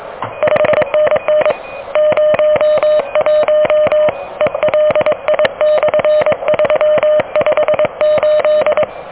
Morse code, heard as a keyed tone, about 24 wpm, 590 Hz. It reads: 5NN 01 ELI/358
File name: OK QRP 2023 / OK QRP 2023